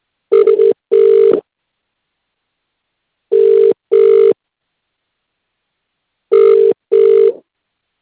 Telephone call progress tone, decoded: ringback tone